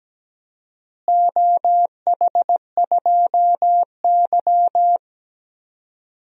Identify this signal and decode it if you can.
Morse code, heard as a keyed tone, about 17 wpm, 700 Hz. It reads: OH2Y
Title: DXCC-100 Call Signs - 4 Characters - Repeated 2x - 17wpm